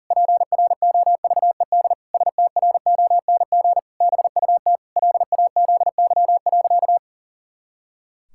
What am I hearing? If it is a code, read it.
Morse code, heard as a keyed tone, about 40 wpm, 700 Hz. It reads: PROVED STRONG BUT LAZY.
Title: Pangram Challenge v2 - 40wpm